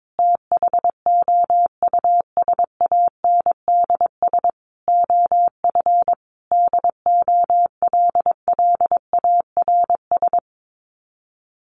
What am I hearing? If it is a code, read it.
Morse code, heard as a keyed tone, about 22 wpm, 700 Hz. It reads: THOUSANDS OF DOLLARS
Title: Sets of 5 Words Encore 22wpm